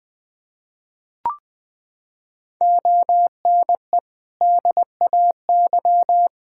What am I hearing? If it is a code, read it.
Morse code, heard as a keyed tone, about 21 wpm, 700 Hz. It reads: ONE DAY
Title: Sentences from Top 1600 Words 20wpm